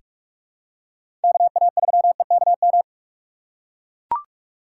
Morse code, heard as a keyed tone, about 44 wpm, 700 Hz. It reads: KA3EKM
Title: USA Call Signs - 6 Characters - 3x Character Spacing - 45wpm